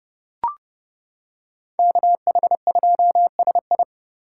Morse code, heard as a keyed tone, about 30 wpm, 700 Hz. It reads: KH2SI